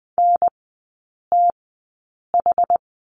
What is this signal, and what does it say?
Morse code, heard as a keyed tone, about 20 wpm, spaced as for 8 wpm, 700 Hz. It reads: NTH
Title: 107 Lesson 18e - Letter m - Three Characters 20wpm